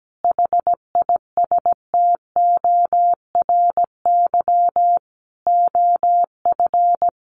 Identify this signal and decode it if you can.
Morse code, heard as a keyed tone, about 17 wpm, 700 Hz. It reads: HISTORY OF
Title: A Sound Of Thunder 17wpm